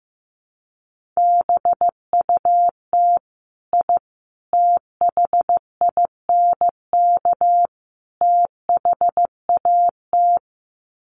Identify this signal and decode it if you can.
Morse code, heard as a keyed tone, about 15 wpm, 700 Hz. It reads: BUT I THINK THAT
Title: Sets of 4 Words 15wpm